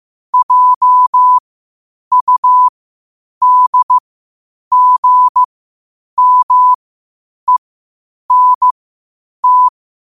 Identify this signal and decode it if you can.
Morse code, heard as a keyed tone, about 15 wpm, spaced as for 8 wpm, 1000 Hz. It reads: JUDGMENT
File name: qotdfiles / 8wpm